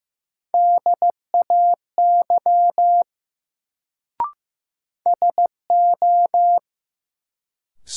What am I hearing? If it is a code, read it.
Morse code, heard as a keyed tone, about 15 wpm, 700 Hz. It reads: DAY SO
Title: Sets of 2 Words - New Word Drill - 15wpm